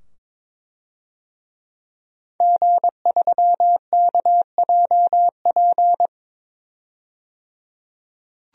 Morse code, heard as a keyed tone, about 22 wpm, 700 Hz. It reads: G3KJP
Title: DXCC-100 Call Signs - 5 Characters - 22wpm